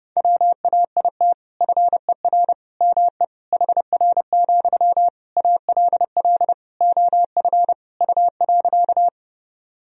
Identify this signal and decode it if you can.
Morse code, heard as a keyed tone, about 32 wpm, 700 Hz. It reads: WAIT FER ME HR, ALL OF U.
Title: Sentences from Top 500 Words and Abbreviations 30wpm